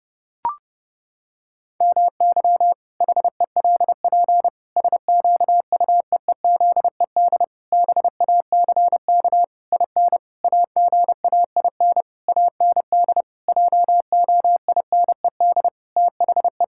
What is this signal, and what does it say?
Morse code, heard as a keyed tone, about 28 wpm, 700 Hz. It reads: MY HELP SQUEEZED BACK IN AGAIN AND JOINED THE
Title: Pangram Challenge v2 - 30wpm